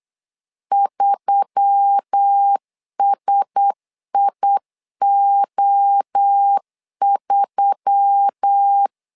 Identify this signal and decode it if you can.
Morse code, heard as a keyed tone, about 8 wpm, 800 Hz. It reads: 3SIO3
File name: LEZIONI / 4 Q1S3 8wpm 3037